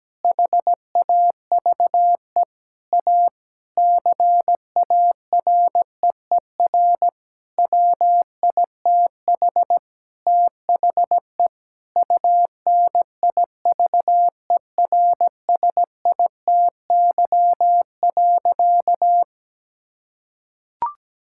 Morse code, heard as a keyed tone, about 17 wpm, 700 Hz. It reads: HAVE A CAREER WITH THE UNIVERSITY.
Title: Sentences from Top 1000 Words - Review Part 6 - 17wpm